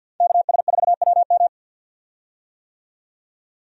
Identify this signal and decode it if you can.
Morse code, heard as a keyed tone, about 51 wpm, 700 Hz. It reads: KS4WM
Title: USA Call Signs - 5 Characters - 50wpm